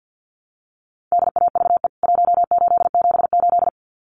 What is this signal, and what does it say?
Morse code, heard as a keyed tone, about 50 wpm, 700 Hz. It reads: DAVE 1878
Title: CWT Exchange 50wpm